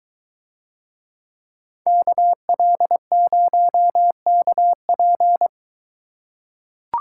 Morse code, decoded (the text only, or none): KL0KP